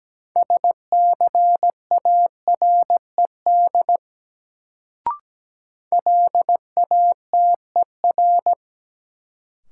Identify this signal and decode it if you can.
Morse code, heard as a keyed tone, about 17 wpm, 700 Hz. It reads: SCARED LATER
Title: Sentences from Top 700 Words - New Word Drill - 17wpm